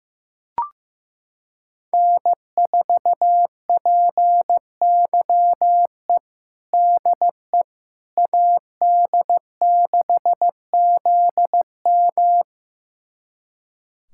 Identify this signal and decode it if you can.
Morse code, heard as a keyed tone, about 15 wpm, 700 Hz. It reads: N4PYE DE AD6ZM
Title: USA Callsign DE Callsign - 5 Characters - 15wpm